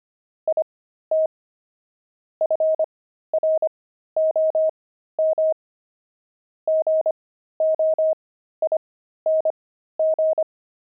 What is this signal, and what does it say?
Morse code, heard as a keyed tone, about 25 wpm, spaced as for 13 wpm, 630 Hz. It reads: IT FROM GOING
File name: pangram-fs-13-cs-25-axle